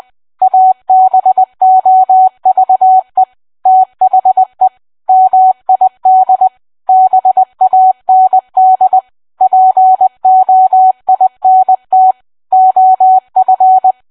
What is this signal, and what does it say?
Morse code, heard as a keyed tone, about 20 wpm, 750 Hz. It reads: ABOVE THE MID BAND POINT OF